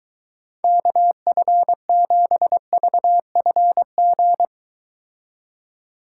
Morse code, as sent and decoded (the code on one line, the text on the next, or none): -.- ..-. --... ...- ..-. --.
KF7VFG